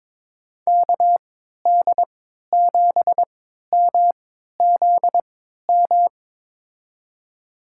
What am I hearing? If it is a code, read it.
Morse code, heard as a keyed tone, about 22 wpm, spaced as for 13 wpm, 700 Hz. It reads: KD7MZM
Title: USA Call Signs - 6 Characters - 3x Character Spacing - 22wpm